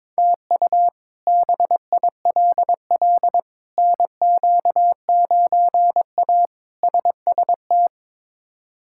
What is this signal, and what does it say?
Morse code, heard as a keyed tone, about 23 wpm, 700 Hz. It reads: TU BILL NQ9A SST
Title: K1USN SST QSO Practice - 22wpm